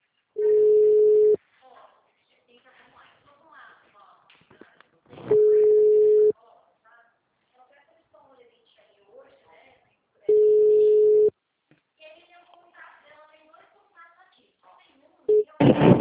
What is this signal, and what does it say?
Ringback tone (Europe, 425 Hz), with speech.
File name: out- -2015- - - .60599